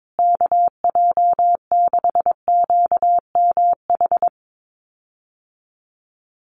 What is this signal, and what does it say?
Morse code, heard as a keyed tone, about 22 wpm, 700 Hz. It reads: KJ6QMH
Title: USA Call Signs Repeated 2x 22wpm